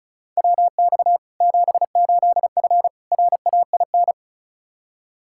Morse code, heard as a keyed tone, about 36 wpm, 700 Hz. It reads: WX 78F RAIN
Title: K1USN SST QSO Practice 35wpm